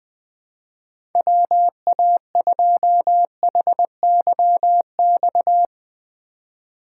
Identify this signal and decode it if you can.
Morse code, heard as a keyed tone, about 21 wpm, 700 Hz. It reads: WA2HYX
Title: USA Call Signs - 6 Characters - 3x Character Spacing - 20wpm